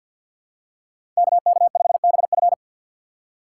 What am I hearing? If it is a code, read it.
Morse code, heard as a keyed tone, about 48 wpm, 700 Hz. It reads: KK5BF
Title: USA Call Signs - 6x Character Spacing - 50wpm